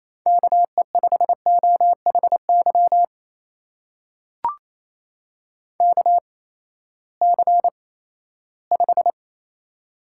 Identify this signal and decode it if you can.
Morse code, heard as a keyed tone, about 28 wpm, spaced as for 16 wpm, 700 Hz. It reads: KE5OHY KC5
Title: USA Call Signs - 6 Characters - 8x Character Spacing - 28wpm